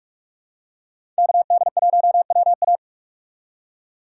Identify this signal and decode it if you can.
Morse code, heard as a keyed tone, about 46 wpm, 700 Hz. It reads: K D 1 W A
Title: USA Call Signs - 5 Characters - 6x Character Spacing - 45wpm